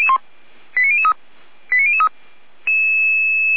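Five-tone selective call (ZVEI), heard as quick runs of five tones, tones of about 70 ms. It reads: 89003 89003
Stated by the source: ZVEI five-tone selective call received by SDR.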